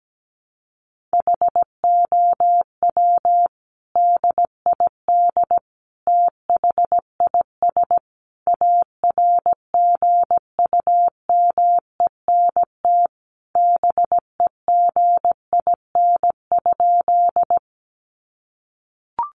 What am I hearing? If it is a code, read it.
Morse code, heard as a keyed tone, about 17 wpm, 700 Hz. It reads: HOW DID THIS ARGUMENT BEGIN?
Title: Sentences from Top 1400 Words 17wpm